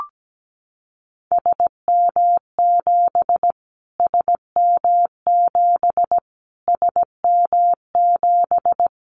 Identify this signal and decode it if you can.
Morse code, heard as a keyed tone, about 17 wpm, 700 Hz. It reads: SM7 SM7 SM7
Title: Callsign Prefixes for Top 25 DXCC - Repeated 3x - 17wpm